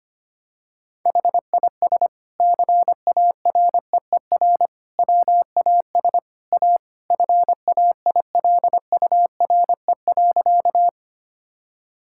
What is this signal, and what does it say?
Morse code, heard as a keyed tone, about 25 wpm, 700 Hz. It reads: HIS CAREER WAS A FAILURE.
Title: Sentences from Top 1000 Words - Review Part 6 - 25wpm